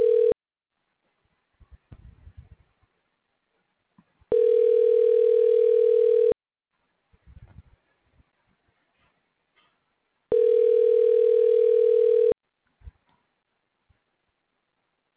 Ringback tone (North America), 440 + 480 Hz.